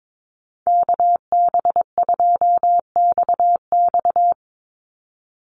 Morse code, heard as a keyed tone, about 22 wpm, 700 Hz. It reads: KB2XX
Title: NA SOTA Chaser Callsigns 22wpm